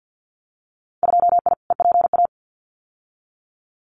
Morse code, heard as a keyed tone, about 50 wpm, 700 Hz. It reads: 2I EPA
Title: Winter Field Day Exchanges Encore 50wpm